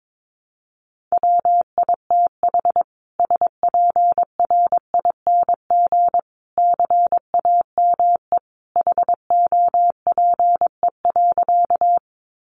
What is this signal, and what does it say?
Morse code, heard as a keyed tone, about 22 wpm, 700 Hz. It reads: WITH SPRING CAME HOPE.
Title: Sentences from Top 1200 Words 22wpm